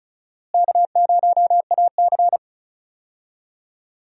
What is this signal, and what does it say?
Morse code, heard as a keyed tone, about 36 wpm, 700 Hz. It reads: K0AC
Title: USA Call Signs - 4 Characters - Repeated 2x - 35wpm